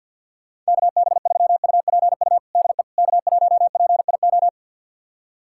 Morse code, heard as a keyed tone, about 48 wpm, 700 Hz. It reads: KB3UPU DE K1PIO
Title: USA Callsign DE Callsign 50wpm